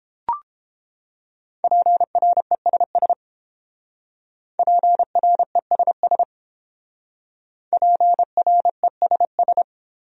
Morse code, heard as a keyed tone, about 29 wpm, 700 Hz. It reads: PRESS PRESS PRESS